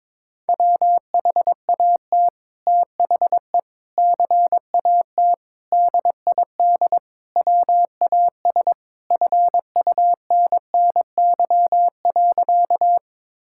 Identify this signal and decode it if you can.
Morse code, heard as a keyed tone, about 21 wpm, 700 Hz. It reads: WHAT THE CAT DID WAS FUNNY.
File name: practice / 22wpm / Sentences with 2-Word Phrases - 22wpm